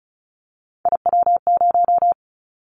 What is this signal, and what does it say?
Morse code, heard as a keyed tone, about 34 wpm, 700 Hz. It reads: IW0
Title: Callsign Prefixes for Top 25 DXCC - Repeated 3x - 35wpm